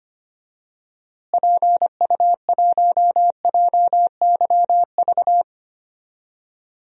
Morse code, heard as a keyed tone, about 25 wpm, 700 Hz. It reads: PU1JYV
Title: DXCC-100 Call Signs - 6 Characters - Repeated 2x - 25wpm